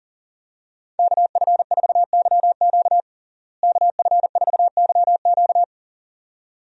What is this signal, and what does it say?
Morse code, heard as a keyed tone, about 40 wpm, 700 Hz. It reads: KF4YQ KF4YQ
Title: USA Call Signs - 5 Characters - Repeated 2x - 40wpm